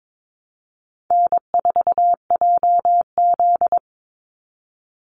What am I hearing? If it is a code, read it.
Morse code, heard as a keyed tone, about 22 wpm, 700 Hz. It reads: N4JZ
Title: USA Call Signs 22wpm